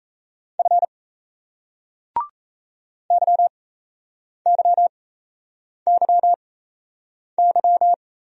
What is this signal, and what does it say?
Morse code, heard as a keyed tone, about 38 wpm, spaced as for 8 wpm, 700 Hz. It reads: F YYYY